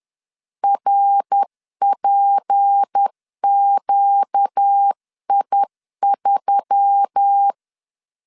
Morse code, heard as a keyed tone, about 10 wpm, 800 Hz. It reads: RPQI3
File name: CW / DET / 1 6 1500C S C 10wpm 4252